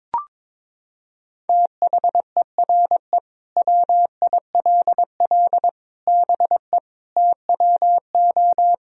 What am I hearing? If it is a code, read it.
Morse code, heard as a keyed tone, about 22 wpm, 700 Hz. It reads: THERE WILL BE TWO
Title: Sentences from Top 1000 Words - Review Part 3 - 22wpm